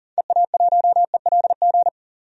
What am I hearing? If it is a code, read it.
Morse code, heard as a keyed tone, about 40 wpm, 700 Hz. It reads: EA1ELG
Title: DXCC-100 Call Signs - 6 Characters - 40wpm